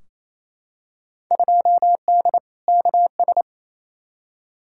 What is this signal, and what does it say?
Morse code, heard as a keyed tone, about 28 wpm, 700 Hz. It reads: 2D KS
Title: Field Day Exchanges Encore - 28wpm